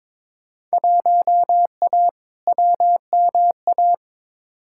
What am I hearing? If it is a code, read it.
Morse code, heard as a keyed tone, about 22 wpm, 700 Hz. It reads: 1A WMA
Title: Field Day Exchanges - 22wpm